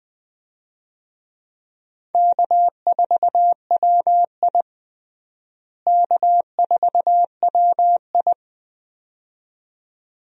Morse code, heard as a keyed tone, about 20 wpm, 700 Hz. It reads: K4WI K4WI
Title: USA Call Signs - 4 Characters - Repeated 2x - 20wpm